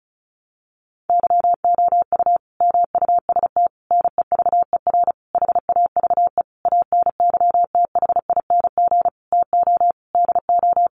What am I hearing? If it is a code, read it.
Morse code, heard as a keyed tone, about 34 wpm, 700 Hz. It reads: YOU MUST NEVER HAVE ANYTHING TO DO
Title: Sentences from Top 1000 Words 35wpm